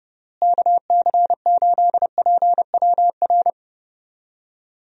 Morse code, heard as a keyed tone, about 30 wpm, 700 Hz. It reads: KC8PWR
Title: USA Call Signs - 6 Characters - Repeated 3x - 30wpm